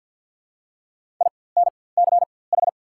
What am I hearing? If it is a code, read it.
Morse code, heard as a keyed tone, about 54 wpm, spaced as for 24 wpm, 690 Hz. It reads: INCH
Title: 4x Character Spacing - Top 500 Words - 50wpm